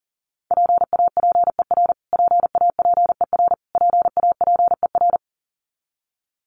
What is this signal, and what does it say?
Morse code, heard as a keyed tone, about 40 wpm, 700 Hz. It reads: PAPER PAPER PAPER
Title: Top 300 Words Repeated 40wpm